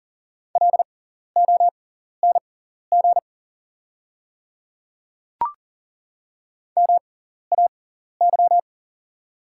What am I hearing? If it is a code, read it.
Morse code, heard as a keyed tone, about 40 wpm, spaced as for 14 wpm, 690 Hz. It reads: LONG MAY
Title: 6x Character Spacing - Top 100 Words - 40wpm